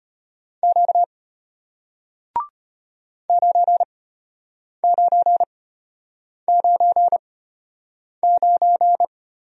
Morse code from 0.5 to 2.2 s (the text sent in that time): Q